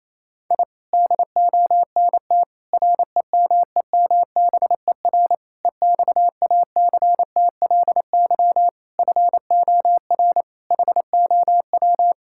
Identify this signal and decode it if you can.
Morse code, heard as a keyed tone, about 28 wpm, 700 Hz. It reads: I DONT REMEMBER EXACTLY FOR HOW
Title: Sentences from Top 1000 Words - Review Part 6 - 28wpm